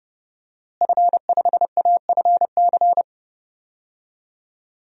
Morse code, heard as a keyed tone, about 30 wpm, 700 Hz. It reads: F5AFC
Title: DXCC-100 Call Signs - 5 Characters - Repeated 3x - 30wpm